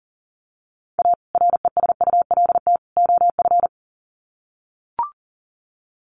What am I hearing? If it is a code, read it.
Morse code, heard as a keyed tone, about 37 wpm, 700 Hz. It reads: A RESULT OF
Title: Sets of 3 Words 40wpm